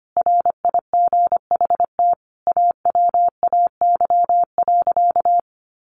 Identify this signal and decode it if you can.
Morse code, heard as a keyed tone, about 25 wpm, 700 Hz. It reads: RIGHT AWAY.
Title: Sentences from Top 1500 Words 25wpm